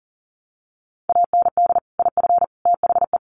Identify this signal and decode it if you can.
Morse code, heard as a keyed tone, about 40 wpm, 700 Hz. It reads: AND IF THE
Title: Sentences from Top 1300 Words 40wpm